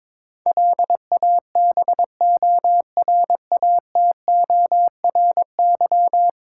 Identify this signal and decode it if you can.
Morse code, heard as a keyed tone, about 22 wpm, 700 Hz. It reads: LABORATORY
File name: practice / 22wpm / 10 Letter Words - First 3 Letters Given - 22wpm